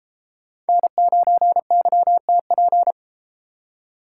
Morse code, heard as a keyed tone, about 31 wpm, 700 Hz. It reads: N9YTP